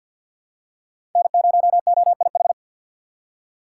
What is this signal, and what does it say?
Morse code, heard as a keyed tone, about 53 wpm, 700 Hz. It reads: N0OIH
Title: USA Call Signs - 5 Characters - 4x Character Spacing - 50wpm